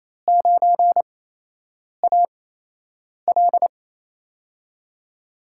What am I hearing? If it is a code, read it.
Morse code, heard as a keyed tone, about 28 wpm, 700 Hz. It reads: 9 A L
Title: USA Call Signs - 4 Characters - 8x Character Spacing - 28wpm